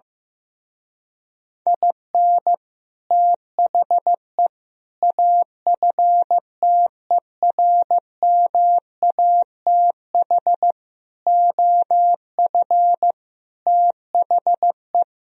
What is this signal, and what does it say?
Morse code, heard as a keyed tone, about 15 wpm, 700 Hz. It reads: IN THE AFTERMATH OF THE
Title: Speed-Racing - Sets of 5 Words 15wpm